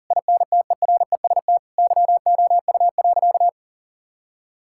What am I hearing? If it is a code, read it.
Morse code, heard as a keyed tone, about 40 wpm, 700 Hz. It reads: INTEREST YOU.
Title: Sentences from Top 1000 Words - Review Part 5 - 40wpm